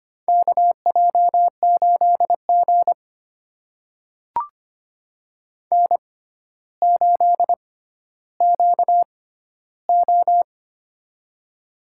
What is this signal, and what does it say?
Morse code, heard as a keyed tone, about 25 wpm, spaced as for 13 wpm, 700 Hz. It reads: KJ8G N8QO
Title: USA Call Signs - 4 Characters - 6x Character Spacing - 25wpm